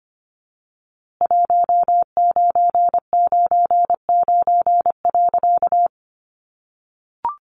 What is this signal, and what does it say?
Morse code, heard as a keyed tone, about 25 wpm, 700 Hz. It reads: 1999.
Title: A Sound Of Thunder 25wpm